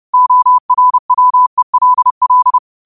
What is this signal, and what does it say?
Morse code, heard as a keyed tone, about 30 wpm, 1000 Hz. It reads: ORWELL